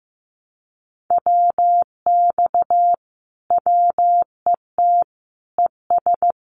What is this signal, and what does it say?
Morse code, heard as a keyed tone, about 15 wpm, 700 Hz. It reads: WX WET ES
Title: Common QSO Phrases Encore 15wpm